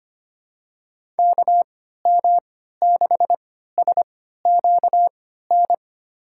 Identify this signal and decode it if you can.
Morse code, heard as a keyed tone, about 25 wpm, spaced as for 14 wpm, 700 Hz. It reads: KM6SQN
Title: USA Call Signs - 6 Characters - 3x Character Spacing - 25wpm